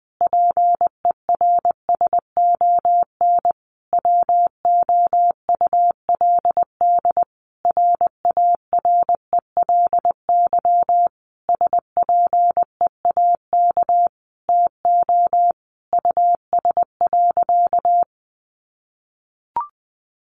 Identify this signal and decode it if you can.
Morse code, heard as a keyed tone, about 20 wpm, 700 Hz. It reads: PERSON WOULD RARELY SPEAK TO US.
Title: Sentences from Top 1400 Words 20wpm